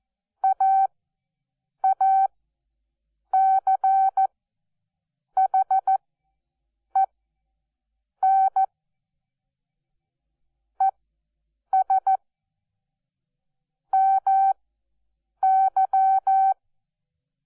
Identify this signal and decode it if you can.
Morse code, heard as a keyed tone, about 14 wpm, spaced as for 7 wpm, 770 Hz. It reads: AACHEN ES MY
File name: Onlinelehrgang / morsen / DJ4UF Morsen12